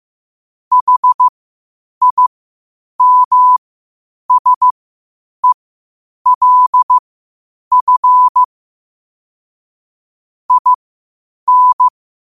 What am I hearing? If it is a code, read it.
Morse code, heard as a keyed tone, about 15 wpm, spaced as for 8 wpm, 1000 Hz. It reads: HIMSELF IN
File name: qotdfiles / 8wpm